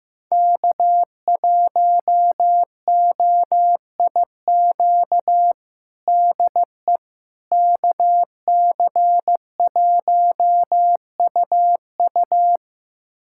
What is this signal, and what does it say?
Morse code, heard as a keyed tone, about 16 wpm, 700 Hz. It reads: K1OIQ DE KC1UU
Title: USA Callsign DE Callsign - 5 Characters - 15wpm